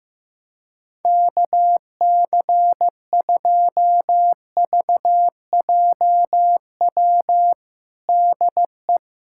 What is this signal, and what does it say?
Morse code, heard as a keyed tone, about 15 wpm, 700 Hz. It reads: KC2VJW DE
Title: USA Callsign DE Callsign 15wpm